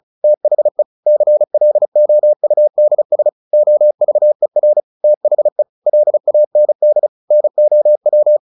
Morse code, heard as a keyed tone, about 34 wpm, 600 Hz. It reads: THE CLOUDS OVER THE LAND NOW
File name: the old man and the sea / the old man and the sea 1 35wpm